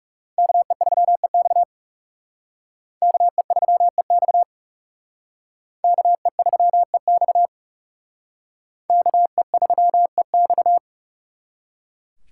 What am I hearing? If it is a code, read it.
Morse code, heard as a keyed tone, about 37 wpm, 700 Hz. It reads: KE3EX KE3EX KE3EX KE3EX
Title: Speed Racing - USA Call Signs 30wpm